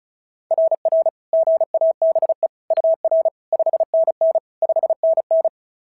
Morse code, heard as a keyed tone, about 36 wpm, 690 Hz. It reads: RR GABE UR 5NN 5NN
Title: SOTA QSO Practice 35wpm